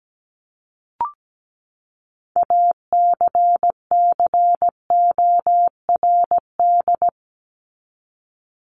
Morse code, heard as a keyed tone, about 17 wpm, 700 Hz. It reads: ACCORD
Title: ING Suffix - Top 100 Words with Base Words 17wpm